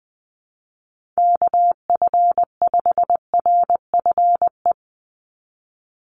Morse code, heard as a keyed tone, about 20 wpm, 700 Hz. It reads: KF5RFE
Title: USA Call Signs 20wpm